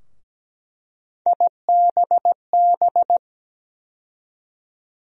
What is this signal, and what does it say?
Morse code, heard as a keyed tone, about 17 wpm, 700 Hz. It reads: IBB
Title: USA Call Sign Postfixes - 3 Letters 17wpm